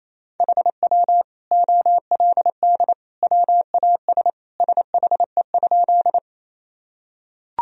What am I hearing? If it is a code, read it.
Morse code, heard as a keyed tone, about 27 wpm, 700 Hz. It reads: HW OLD WAS SHE?
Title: Sentences from Top 200 Words and Abbreviations 28wpm